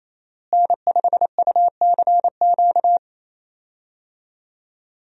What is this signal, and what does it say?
Morse code, heard as a keyed tone, about 28 wpm, 700 Hz. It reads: N5UCQ
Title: USA Call Signs - 5 Characters - Repeated 3x - 28wpm